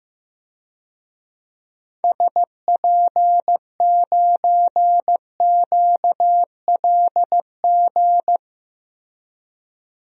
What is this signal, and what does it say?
Morse code, heard as a keyed tone, about 15 wpm, 700 Hz. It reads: SP9QLG
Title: DXCC-100 Call Signs - 6 Characters - 15wpm